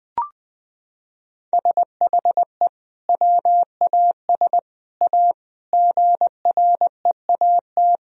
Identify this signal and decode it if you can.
Morse code, heard as a keyed tone, about 20 wpm, 700 Hz. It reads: SHE WAS A GREAT
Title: Sentences from Top 1000 Words - Review Part 4 - 20wpm